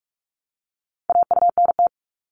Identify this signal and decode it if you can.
Morse code, heard as a keyed tone, about 46 wpm, 700 Hz. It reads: AUNT
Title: Multi-letter Phonemes Repeated 3x 45wpm